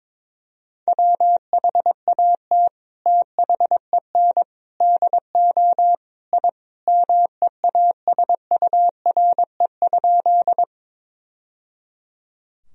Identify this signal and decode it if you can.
Morse code, heard as a keyed tone, about 22 wpm, 700 Hz. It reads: WHAT THEN DO I MEASURE?
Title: Sentences from Top 1000 Words - Review Part 2 - 22wpm